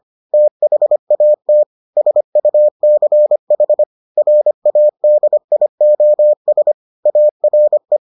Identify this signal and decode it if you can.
Morse code, heard as a keyed tone, about 25 wpm, 600 Hz. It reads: THAT SUCH RADIOS ARE